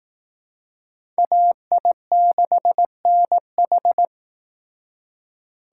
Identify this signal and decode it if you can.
Morse code, heard as a keyed tone, about 19 wpm, 700 Hz. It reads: AI6NH